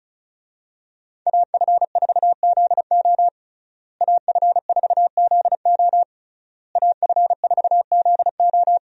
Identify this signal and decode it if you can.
Morse code, heard as a keyed tone, about 36 wpm, 700 Hz. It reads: AF4ZO AF4ZO AF4ZO
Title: USA Call Signs - 5 Characters - Repeated 3x - 35wpm